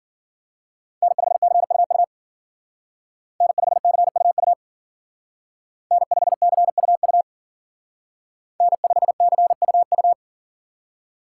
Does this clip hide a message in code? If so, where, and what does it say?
Yes: Morse code from 1.0 to 10.1 s, heard as a keyed tone, about 52 wpm, 700 Hz. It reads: N5CUU N5CUU N5CUU N5CUU